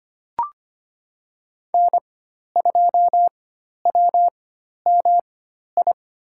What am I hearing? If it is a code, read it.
Morse code, heard as a keyed tone, about 25 wpm, 700 Hz. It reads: N 2 W M I
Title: USA Call Signs - 4x Character Spacing - 25wpm